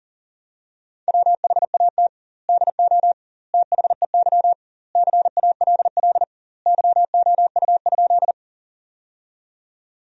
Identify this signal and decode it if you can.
Morse code, heard as a keyed tone, about 40 wpm, 700 Hz. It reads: WHAT DO THEY CALL YOU?
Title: 2x Word-Spacing - Sentences from Top 100 Words - 40wpm